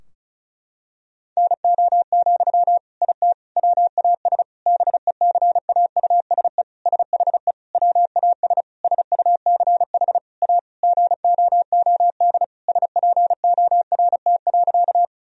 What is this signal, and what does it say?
Morse code, heard as a keyed tone, about 36 wpm, 700 Hz. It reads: NO, IT WAS BECAUSE SHE WAS SUCH A GOOD SPORT.
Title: Sentences from Top 1000 Words - Review Part 4 - 35wpm